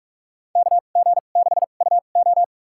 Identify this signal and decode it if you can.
Morse code, heard as a keyed tone, about 42 wpm, spaced as for 31 wpm, 690 Hz. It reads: KG6UO
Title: USA Call Signs - 5 Characters - 2x Character Spacing - 45wpm